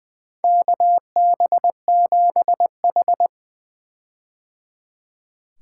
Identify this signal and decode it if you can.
Morse code, heard as a keyed tone, about 20 wpm, 700 Hz. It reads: KB7H
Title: K1USN SST QSO Practice - 20wpm